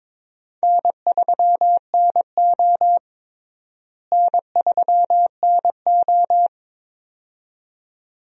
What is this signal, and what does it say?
Morse code, heard as a keyed tone, about 22 wpm, 700 Hz. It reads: N3NO N3NO
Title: USA Call Signs - 4 Characters - Repeated 2x - 22wpm